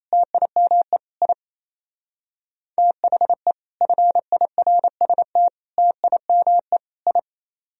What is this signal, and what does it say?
Morse code, heard as a keyed tone, about 33 wpm, 700 Hz. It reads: TIME I THE FIRST TIME I
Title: Speed-Racing - Sets of 4 Words 28wpm